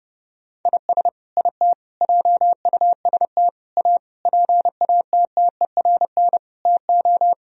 Morse code, heard as a keyed tone, about 30 wpm, 700 Hz. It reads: IS IT JUST A PATTERN TO
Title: Sentences from Top 1000 Words - Review Part 2 - 30wpm